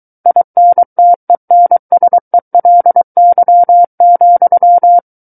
Morse code, heard as a keyed tone, about 23 wpm, 700 Hz. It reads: INTENSELY,
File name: Book One / 14-Chapter 14 - 23 WPM